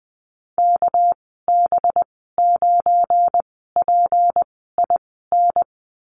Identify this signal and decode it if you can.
Morse code, heard as a keyed tone, about 20 wpm, spaced as for 14 wpm, 700 Hz. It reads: KB9PIN
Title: 200 Lesson 32g - Letter k - Call Signs 20wpm